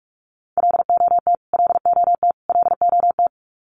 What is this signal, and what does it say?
Morse code, heard as a keyed tone, about 44 wpm, 700 Hz. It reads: LOT LOT LOT
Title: Top 500 Words Repeated 45wpm